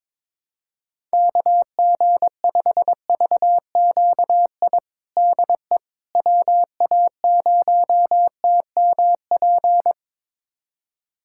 Morse code, heard as a keyed tone, about 22 wpm, 700 Hz. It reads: KG5VQI DE WA0TMP
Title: USA Callsign DE Callsign 22wpm